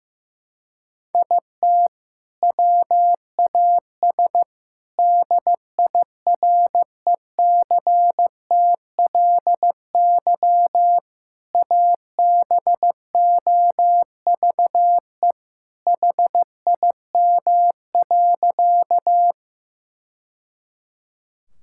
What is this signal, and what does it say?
Morse code, heard as a keyed tone, about 16 wpm, 700 Hz. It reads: IT WAS DIRECTLY ABOVE HIM.
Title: Sentences from Top 1000 Words - Review Part 5 - 15wpm